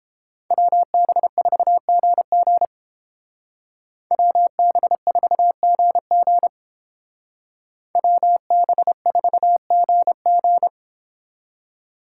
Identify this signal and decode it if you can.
Morse code, heard as a keyed tone, about 30 wpm, 700 Hz. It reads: WB4GG WB4GG WB4GG